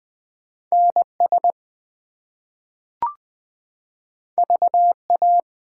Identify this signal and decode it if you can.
Morse code, heard as a keyed tone, about 20 wpm, 700 Hz. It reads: NS VA
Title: ARRL Sections v2 - 20wpm